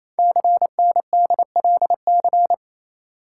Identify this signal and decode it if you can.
Morse code, heard as a keyed tone, about 28 wpm, 700 Hz. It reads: CNDLC
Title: Code Groups - 5 Letters - 28wpm